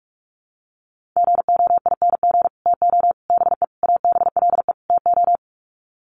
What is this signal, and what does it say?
Morse code, heard as a keyed tone, about 44 wpm, 700 Hz. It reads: GOING TO BE ABLE TO
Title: Sets of 5 Words 45wpm